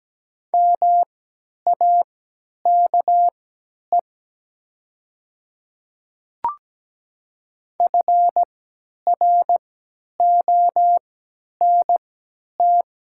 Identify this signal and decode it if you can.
Morse code, heard as a keyed tone, about 17 wpm, spaced as for 10 wpm, 700 Hz. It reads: MAKE FRONT
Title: 3x Character Spacing - Top 500 Words - 17wpm